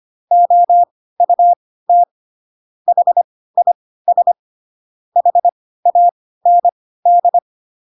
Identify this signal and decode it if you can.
Morse code, heard as a keyed tone, about 25 wpm, spaced as for 16 wpm, 700 Hz. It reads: OUT HIS HAND